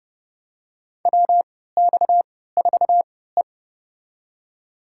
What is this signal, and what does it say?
Morse code, heard as a keyed tone, about 32 wpm, spaced as for 17 wpm, 700 Hz. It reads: WX4E
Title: USA Call Signs - 4 Characters - 3x Character Spacing - 30wpm